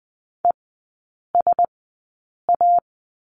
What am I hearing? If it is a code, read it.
Morse code, heard as a keyed tone, about 20 wpm, spaced as for 8 wpm, 700 Hz. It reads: ESA
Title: 035 Lesson 7e - Letter s - Three Characters 20wpm